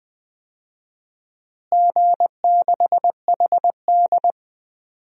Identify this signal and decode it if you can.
Morse code, heard as a keyed tone, about 20 wpm, 700 Hz. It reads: G6HD
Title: DXCC-100 Call Signs - 4 Characters - Repeated 2x - 20wpm